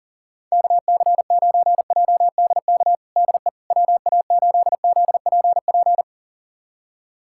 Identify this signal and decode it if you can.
Morse code, heard as a keyed tone, about 40 wpm, 700 Hz. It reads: KC9JDK DE WA8ZPP
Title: USA Callsign DE Callsign - 6 Characters - 40wpm